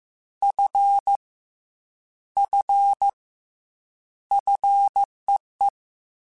Morse code, heard as a keyed tone, about 15 wpm, 800 Hz. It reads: F F FEE
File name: F and L / F and L mp3 / F and L Prefixes 3x2 Twice @13